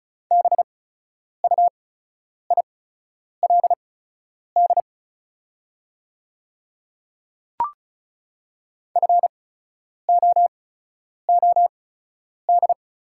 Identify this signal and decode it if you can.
Morse code, heard as a keyed tone, about 36 wpm, spaced as for 10 wpm, 690 Hz. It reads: BUILD FOOD
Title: 8x Character Spacing - Top 500 Words - 35wpm